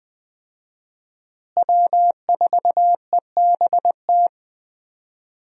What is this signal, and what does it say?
Morse code, heard as a keyed tone, about 20 wpm, 700 Hz. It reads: W4EBT
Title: USA Call Signs - 5 Characters - Repeated 3x - 20wpm